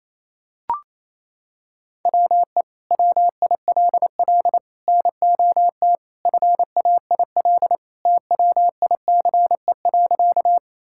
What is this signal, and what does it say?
Morse code, heard as a keyed tone, about 27 wpm, 700 Hz. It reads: WE WILL NOT FAIL TWICE.
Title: Sentences from Top 1600 Words 28wpm